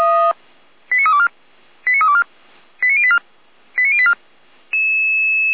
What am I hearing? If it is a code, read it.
Five-tone selective call (ZVEI), heard as quick runs of five tones, tones of about 80 ms. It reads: 89324 89324 89084 89084